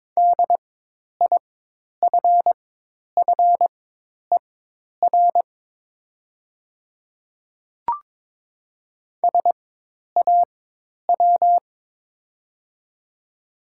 Morse code, heard as a keyed tone, about 22 wpm, spaced as for 10 wpm, 700 Hz. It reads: DIFFER SAW
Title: 4x Character Spacing - Top 500 Words - 22wpm